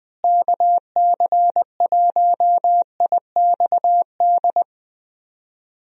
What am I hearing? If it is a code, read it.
Morse code, heard as a keyed tone, about 20 wpm, 700 Hz. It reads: KC1IXD
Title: USA Call Signs - 6 Characters - Repeated 3x - 20wpm